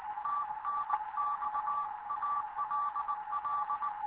Morse code, heard as a keyed tone, about 19 wpm, 1160 Hz. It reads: TNX ALL
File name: Morze-HA2MN / hangp / 2 11 GYAKORLAT